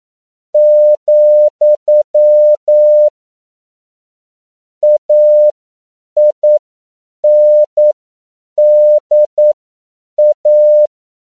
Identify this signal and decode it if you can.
Morse code, heard as a keyed tone, about 9 wpm, spaced as for 7 wpm, 600 Hz. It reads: , AINDA
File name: CW / 10PPM